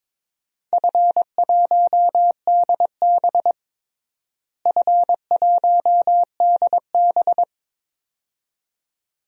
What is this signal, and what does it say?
Morse code, heard as a keyed tone, about 21 wpm, 700 Hz. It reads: F1DB F1DB
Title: DXCC-100 Call Signs - 4 Characters - Repeated 2x - 22wpm